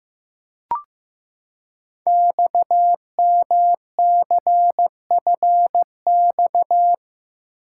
Morse code, heard as a keyed tone, about 15 wpm, 700 Hz. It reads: XMCFX
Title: Code Groups - 5 Letters - 15wpm